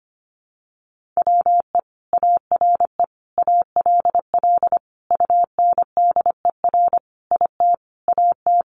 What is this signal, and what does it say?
Morse code, heard as a keyed tone, about 25 wpm, 700 Hz. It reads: WE ARE ALL UNDER IT AT
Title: Sentences from Top 1500 Words 25wpm